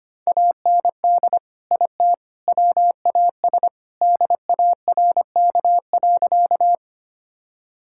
Morse code, heard as a keyed tone, about 25 wpm, 700 Hz. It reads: AND IT WAS DARK.
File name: practice / 25wpm / Sentences with 3-Word Phrases - 25wpm